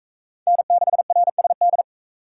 Morse code, heard as a keyed tone, about 43 wpm, 700 Hz. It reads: N6RSD